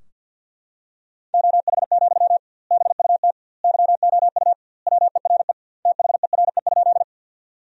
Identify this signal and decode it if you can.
Morse code, heard as a keyed tone, about 53 wpm, 700 Hz. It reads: OH, BUT YOU WERE THERE?
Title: 2x Word-Spacing - Sentences from Top 500 Words - 50wpm